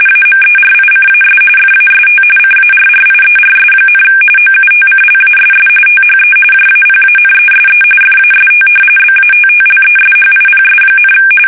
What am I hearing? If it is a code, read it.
RTTY, about 45.45 baud. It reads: AMERICAS FAST CARRIERS, OUTCLASS THE JAPANESE FAST BATTLESHIPS,
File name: fsk850 / battleshipnj